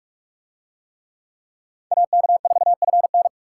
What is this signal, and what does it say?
Morse code, heard as a keyed tone, about 41 wpm, 700 Hz. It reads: AK4FN